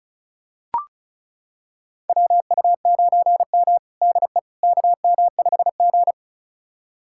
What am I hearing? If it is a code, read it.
Morse code, heard as a keyed tone, about 36 wpm, 700 Hz. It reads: WU9M DE KM5G
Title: USA Callsign DE Callsign - 4 Characters - 35wpm